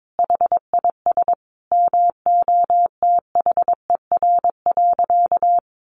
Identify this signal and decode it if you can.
Morse code, heard as a keyed tone, about 22 wpm, 700 Hz. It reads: HIS MOTHER.
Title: Sentences from Top 1200 Words 22wpm